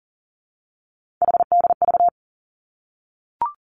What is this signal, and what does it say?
Morse code, heard as a keyed tone, about 40 wpm, 700 Hz. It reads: HDV
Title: Three Challenging Letter-Number Combinations 40wpm